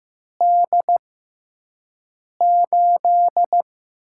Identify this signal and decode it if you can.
Morse code, heard as a keyed tone, about 15 wpm, spaced as for 5 wpm, 700 Hz. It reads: D8
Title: USA Call Signs - 6x Character Spacing - 15wpm